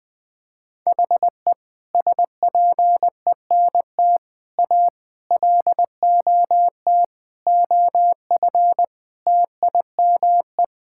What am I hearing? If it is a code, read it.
Morse code, heard as a keyed tone, about 20 wpm, 700 Hz. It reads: HE SPENT A LOT OF TIME
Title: Sentences with 5-Word Phrases - Spoken Beginning - 20wpm